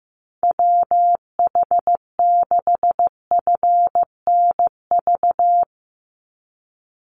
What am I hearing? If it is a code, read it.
Morse code, heard as a keyed tone, about 15 wpm, 700 Hz. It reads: WH6FNV
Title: USA Call Signs 15wpm